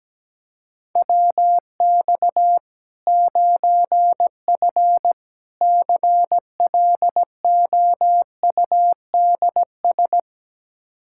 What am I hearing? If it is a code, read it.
Morse code, heard as a keyed tone, about 17 wpm, 700 Hz. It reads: WX 9F CLOUDS
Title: K1USN SST QSO Practice 17wpm